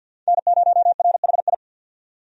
Morse code, heard as a keyed tone, about 50 wpm, 700 Hz. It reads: N0RHI
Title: USA Call Signs - 5 Characters - 50wpm